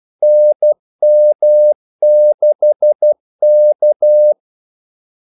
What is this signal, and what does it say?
Morse code, heard as a keyed tone, about 12 wpm, 600 Hz. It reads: NM6K
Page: 2 ekstra ordmellemrum mellem kaldesignalerne